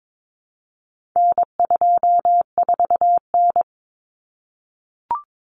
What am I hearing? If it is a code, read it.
Morse code, heard as a keyed tone, about 22 wpm, 700 Hz. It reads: N24N
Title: Common Cut Numbers - 4-Digit 22wpm